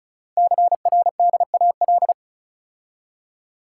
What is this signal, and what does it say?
Morse code, heard as a keyed tone, about 36 wpm, 700 Hz. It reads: CRDAL
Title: Code Groups - 5 Letters - 35wpm